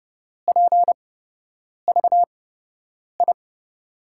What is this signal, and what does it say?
Morse code, heard as a keyed tone, about 32 wpm, spaced as for 8 wpm, 700 Hz. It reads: PVI
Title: USA Call Signs - 6 Characters - 8x Character Spacing - 30wpm